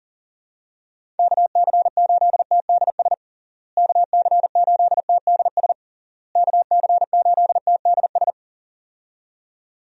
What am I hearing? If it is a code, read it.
Morse code, heard as a keyed tone, about 40 wpm, 700 Hz. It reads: KC8TDS KC8TDS KC8TDS
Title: USA Call Signs - 6 Characters - Repeated 3x - 40wpm